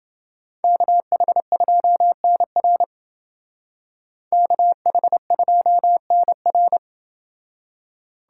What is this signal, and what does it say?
Morse code, heard as a keyed tone, about 29 wpm, 700 Hz. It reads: KH2NR KH2NR